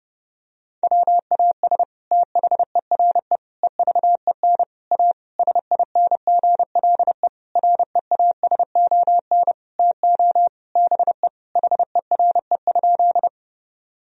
Morse code, heard as a keyed tone, about 30 wpm, 700 Hz. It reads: WAS THERE EVEN A SINGLE REASON TO BE HERE?
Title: Sentences from Top 1000 Words - Review Part 4 - 30wpm